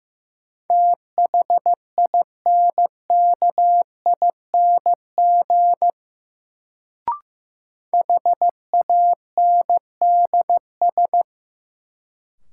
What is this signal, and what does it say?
Morse code, heard as a keyed tone, about 15 wpm, 700 Hz. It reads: THINKING HANDS
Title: Sentences from Top 1000 Words - Review Part 6 - New Word Drill - 15wpm